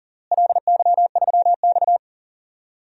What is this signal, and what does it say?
Morse code, heard as a keyed tone, about 37 wpm, 700 Hz. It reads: LY3X
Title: DXCC-100 Call Signs - 4 Characters - Repeated 3x - 40wpm